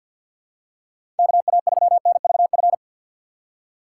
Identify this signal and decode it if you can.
Morse code, heard as a keyed tone, about 53 wpm, 700 Hz. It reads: KA3NVF
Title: USA Call Signs - 6 Characters - 6x Character Spacing - 50wpm